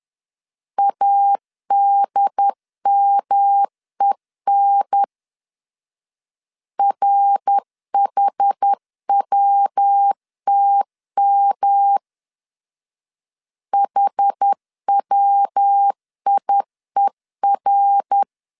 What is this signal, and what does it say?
Morse code, heard as a keyed tone, about 10 wpm, 800 Hz. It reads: ADMEN RHWTM HWIER
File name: CW / DET / 1 3 1500C S B 10wpm 3335